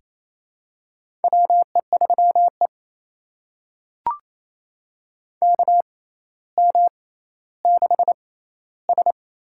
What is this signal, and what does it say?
Morse code, heard as a keyed tone, about 28 wpm, spaced as for 14 wpm, 700 Hz. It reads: WE3E KM6S
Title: USA Call Signs - 6x Character Spacing - 28wpm